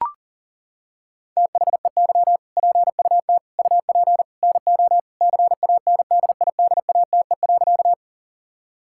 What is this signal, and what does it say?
Morse code, heard as a keyed tone, about 37 wpm, 700 Hz. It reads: THEY PUT UP NO CANDIDATE.
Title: Sentences from Top 1600 Words 40wpm